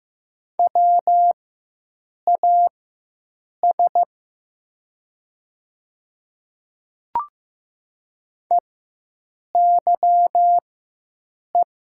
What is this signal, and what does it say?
Morse code, heard as a keyed tone, about 15 wpm, spaced as for 7 wpm, 700 Hz. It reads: WAS EYE
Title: 4x Character Spacing - Top 300 Words - 15wpm